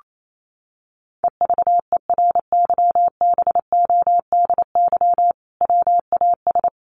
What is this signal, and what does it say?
Morse code, heard as a keyed tone, about 28 wpm, 700 Hz. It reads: EVERYBODY WAS
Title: Sentences from Top 1400 Words 28wpm